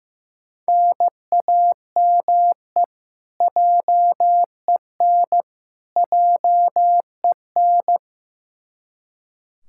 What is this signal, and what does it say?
Morse code, heard as a keyed tone, about 15 wpm, 700 Hz. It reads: NAME JEN JEN
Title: K1USN SST QSO Practice 15wpm